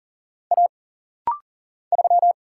Morse code, heard as a keyed tone, about 40 wpm, 700 Hz. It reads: A 3
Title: Single Letter-Number Alt - Rapid-Fire 40wpm